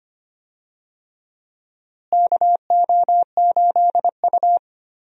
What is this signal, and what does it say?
Morse code, heard as a keyed tone, about 25 wpm, 700 Hz. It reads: KO8U
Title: USA Call Signs - 4 Characters - Repeated 3x - 25wpm